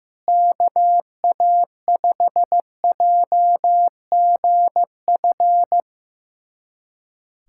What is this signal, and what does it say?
Morse code, heard as a keyed tone, about 15 wpm, 700 Hz. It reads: KA5JGF
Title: USA Callsign DE Callsign - 6 Characters - 15wpm